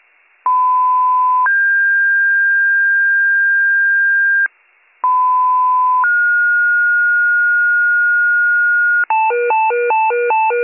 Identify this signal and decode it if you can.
Two-tone page, heard as a one-second tone, then a three-second tone, about 1010 then 1600 Hz, about 1010 then 1430 Hz, with noise. It is heard from 0.5 to 9.0 s.